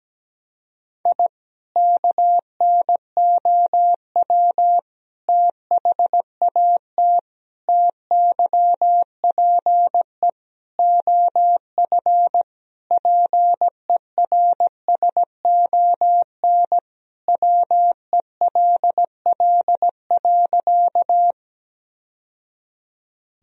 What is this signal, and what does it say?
Morse code, heard as a keyed tone, about 17 wpm, 700 Hz. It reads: I KNOW THAT TYPE OF PERSON WELL.
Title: Sentences from Top 1000 Words - Review Part 3 - 17wpm